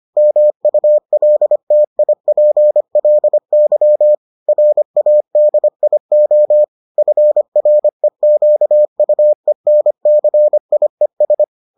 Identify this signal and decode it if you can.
Morse code, heard as a keyed tone, about 25 wpm, 600 Hz. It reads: MULTIPLY RADIO FREQUENCIES